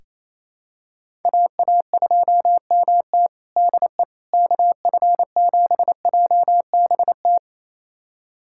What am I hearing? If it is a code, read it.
Morse code, heard as a keyed tone, about 28 wpm, 700 Hz. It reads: AA2MT DE KF7JBT
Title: USA Callsign DE Callsign 28wpm